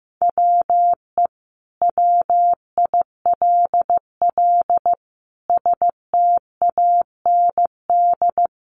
Morse code, heard as a keyed tone, about 15 wpm, 700 Hz. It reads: WE WILL STAND
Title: Sentences from Top 1400 Words 15wpm